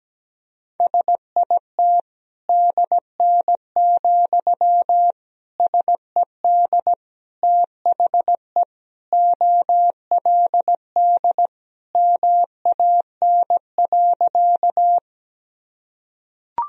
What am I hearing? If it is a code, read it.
Morse code, heard as a keyed tone, about 17 wpm, 700 Hz. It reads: SIT DN, SED THE OLD MAN.
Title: Sentences from Top 500 Words and Abbreviations 17wpm